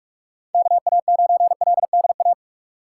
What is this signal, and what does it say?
Morse code, heard as a keyed tone, about 46 wpm, 700 Hz. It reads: KA9LDA
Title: USA Call Signs - 6 Characters - 4x Character Spacing - 45wpm